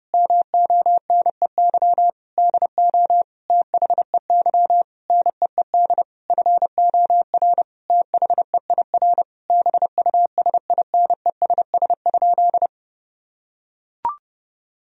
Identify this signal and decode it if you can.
Morse code, heard as a keyed tone, about 30 wpm, 700 Hz. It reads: MONEY DO THEY NEED FOR THEIR BUSINESS?
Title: Sentences from Top 1000 Words - Review Part 4 - 30wpm